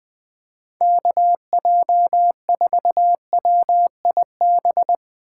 Morse code, heard as a keyed tone, about 20 wpm, 700 Hz. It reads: KJ4WIB
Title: Speed Racing - USA 2x3 Call Signs 17wpm